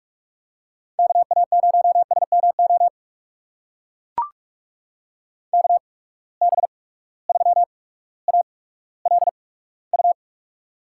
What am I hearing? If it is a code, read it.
Morse code, heard as a keyed tone, about 44 wpm, spaced as for 19 wpm, 700 Hz. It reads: KA0SMO KB3ALU
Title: USA Call Signs - 6 Characters - 8x Character Spacing - 45wpm